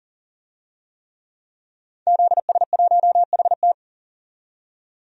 Morse code, heard as a keyed tone, about 40 wpm, 700 Hz. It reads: ZS1HT
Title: DXCC-100 Call Signs - 5 Characters - 40wpm